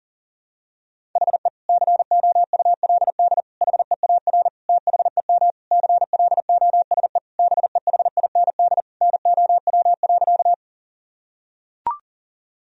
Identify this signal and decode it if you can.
Morse code, heard as a keyed tone, about 40 wpm, 700 Hz. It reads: HE COULD HEAR THEM CLOSE BEHIND NOW.
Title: Sentences from Top 1000 Words - Review Part 1 - 40wpm